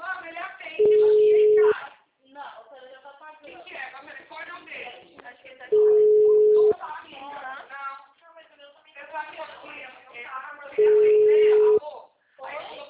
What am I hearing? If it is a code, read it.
Ringback tone (Europe, 425 Hz), with speech.